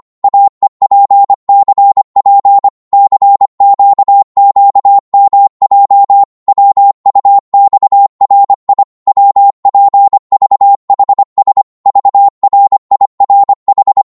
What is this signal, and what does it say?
Morse code, heard as a keyed tone, about 25 wpm, 800 Hz. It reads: AEPCP CQQMJ WUXRI WPVHS VRIRH
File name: Morse / ltr-25wpm